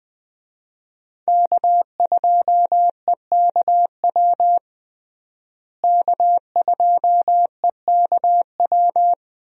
Morse code, heard as a keyed tone, about 20 wpm, 700 Hz. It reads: K2EKW K2EKW
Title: USA Call Signs - 5 Characters - Repeated 3x - 20wpm